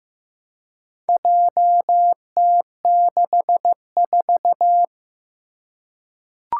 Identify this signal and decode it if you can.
Morse code, heard as a keyed tone, about 15 wpm, 700 Hz. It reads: JT64
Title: Four Challenging Letter-Number Combinations 15wpm